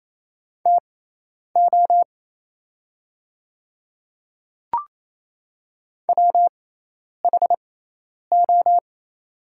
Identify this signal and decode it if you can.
Morse code, heard as a keyed tone, about 28 wpm, spaced as for 10 wpm, 700 Hz. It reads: TO WHO
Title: 6x Character Spacing - Top 100 Words - 28wpm